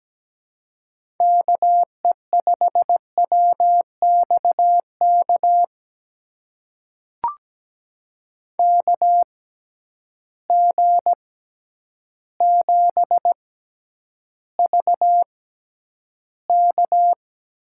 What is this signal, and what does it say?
Morse code, heard as a keyed tone, about 17 wpm, spaced as for 9 wpm, 700 Hz. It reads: KE5WXK KG7VK
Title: USA Call Signs - 6 Characters - 6x Character Spacing - 17wpm